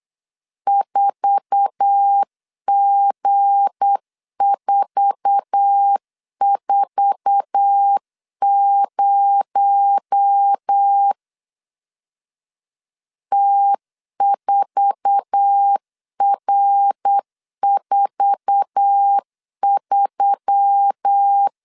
Morse code, heard as a keyed tone, about 8 wpm, 800 Hz. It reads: 4G440 T4R43
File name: LEZIONI / 8 YJ45 / 8 YJ4 8wpm 3246